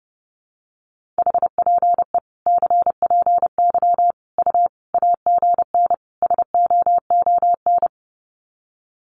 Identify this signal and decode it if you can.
Morse code, heard as a keyed tone, about 30 wpm, 700 Hz. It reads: HPE CPY U AGN SOON
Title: Common QSO Phrases 30wpm